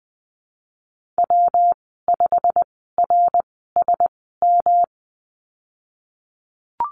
Morse code, heard as a keyed tone, about 19 wpm, spaced as for 14 wpm, 700 Hz. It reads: W5RSM
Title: 179 Lesson 29g - Slash - Call Signs 20wpm